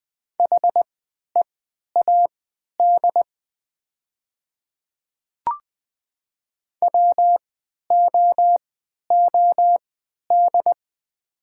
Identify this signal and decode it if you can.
Morse code, heard as a keyed tone, about 20 wpm, spaced as for 11 wpm, 700 Hz. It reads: HEAD WOOD
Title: 3x Character Spacing - Top 300 Words - 20wpm